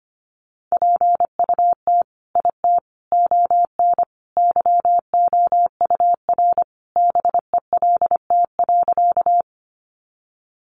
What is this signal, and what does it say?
Morse code, heard as a keyed tone, about 25 wpm, 700 Hz. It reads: PUT IT ON YOUR BELT.
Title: Sentences from Top 1400 Words 25wpm